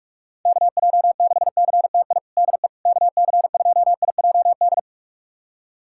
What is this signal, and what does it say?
Morse code, heard as a keyed tone, about 46 wpm, 690 Hz. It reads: KJ6CTI DE KC2IJD
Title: USA Callsign DE Callsign - 6 Characters - 45wpm